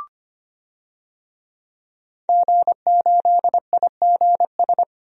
Morse code, heard as a keyed tone, about 25 wpm, 700 Hz. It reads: G8IGS
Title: DXCC-100 Call Signs - 5 Characters - 25wpm